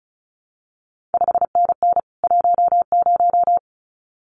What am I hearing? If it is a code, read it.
Morse code, heard as a keyed tone, about 36 wpm, 700 Hz. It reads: 5NN 10
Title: CQ WW DX Contest Exchange Encore - 35wpm